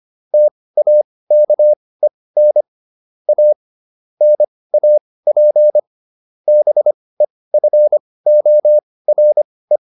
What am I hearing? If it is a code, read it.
Morse code, heard as a keyed tone, about 25 wpm, spaced as for 18 wpm, 600 Hz. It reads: TAKEN A NAP BEFORE